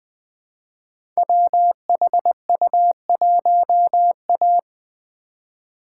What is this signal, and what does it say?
Morse code, heard as a keyed tone, about 20 wpm, 700 Hz. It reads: WHU1A
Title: Code Groups - 5 Letter:Numbers - 20wpm